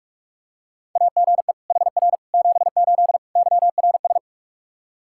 Morse code, heard as a keyed tone, about 46 wpm, 700 Hz. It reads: AGE HR 78 YRS
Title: Practice QSO - Exchange 3 - More Info - 45wpm